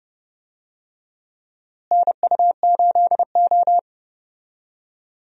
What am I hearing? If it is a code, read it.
Morse code, heard as a keyed tone, about 31 wpm, 700 Hz. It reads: NU8O
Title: USA Call Signs - 4 Characters - Repeated 3x - 30wpm